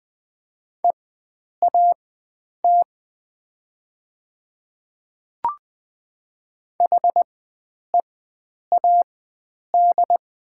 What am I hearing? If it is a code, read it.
Morse code, heard as a keyed tone, about 20 wpm, spaced as for 9 wpm, 700 Hz. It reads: EAT HEAD
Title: 4x Character Spacing - Top 300 Words - 20wpm